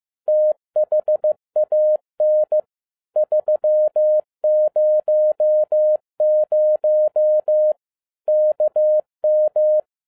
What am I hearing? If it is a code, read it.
Morse code, heard as a keyed tone, about 15 wpm, 610 Hz. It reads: THAN 300 KM
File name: FIELD DAY CONTEST QRV FROM SAKAE VILLAGE 15wpm